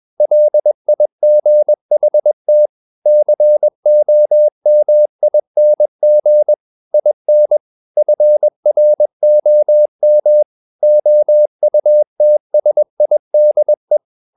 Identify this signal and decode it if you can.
Morse code, heard as a keyed tone, about 21 wpm, 600 Hz. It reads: LIGHT COMING IN FROM OUTSIDE
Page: CW med Gnister